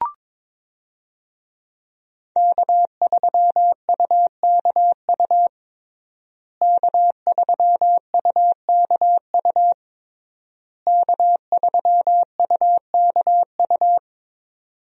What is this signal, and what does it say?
Morse code, heard as a keyed tone, about 22 wpm, 700 Hz. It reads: K3UKU K3UKU K3UKU
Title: USA Call Signs - 5 Characters - Repeated 3x - 22wpm